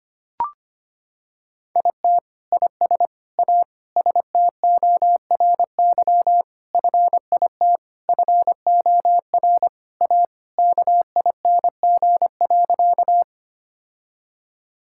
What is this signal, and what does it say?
Morse code, heard as a keyed tone, about 25 wpm, 700 Hz. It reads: IT IS A STORY FIT FOR A KING.
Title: Sentences from Top 1000 Words - Review Part 5 - 25wpm